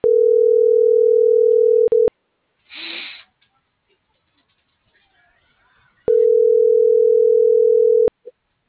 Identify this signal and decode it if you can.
Ringback tone (North America), 440 + 480 Hz.